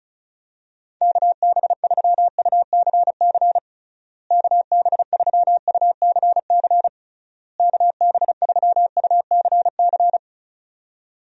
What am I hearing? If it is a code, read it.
Morse code, heard as a keyed tone, about 34 wpm, 700 Hz. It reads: KB3UCC KB3UCC KB3UCC
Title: USA Call Signs - 6 Characters - Repeated 3x - 35wpm